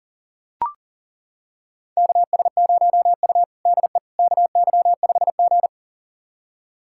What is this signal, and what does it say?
Morse code, heard as a keyed tone, about 40 wpm, 700 Hz. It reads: KS0U DE KY5G
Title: USA Callsign DE Callsign - 4 Characters - 40wpm